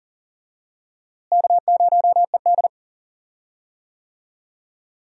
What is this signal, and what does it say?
Morse code, heard as a keyed tone, about 40 wpm, 700 Hz. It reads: K0ED
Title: USA Call Signs - 4 Characters - Repeated 3x - 40wpm